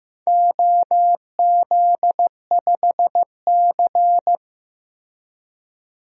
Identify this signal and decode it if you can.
Morse code, heard as a keyed tone, about 16 wpm, 700 Hz. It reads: OZ5C
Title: DXCC-100 Call Signs - 4 Characters - Repeated 2x - 15wpm